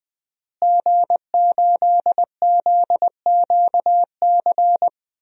Morse code, heard as a keyed tone, about 21 wpm, 700 Hz. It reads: G8ZQC
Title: DXCC-100 Call Signs - 5 Characters - Repeated 3x - 20wpm